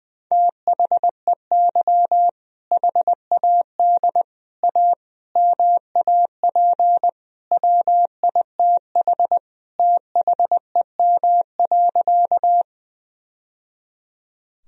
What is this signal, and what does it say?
Morse code, heard as a keyed tone, about 21 wpm, 700 Hz. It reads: THEY HAD A MAP WITH THEM.
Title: Sentences from Top 1000 Words - Review Part 2 - 20wpm